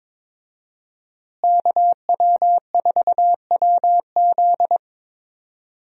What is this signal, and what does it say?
Morse code, heard as a keyed tone, about 22 wpm, 700 Hz. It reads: KW4WZ
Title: USA Call Signs - 5 Characters - Repeated 2x - 22wpm